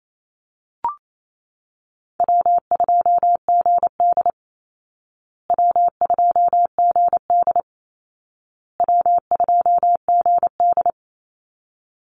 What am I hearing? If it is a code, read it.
Morse code, heard as a keyed tone, about 28 wpm, 700 Hz. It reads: W2GD W2GD W2GD
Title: Easy Call Signs Repeated 3x - 28wpm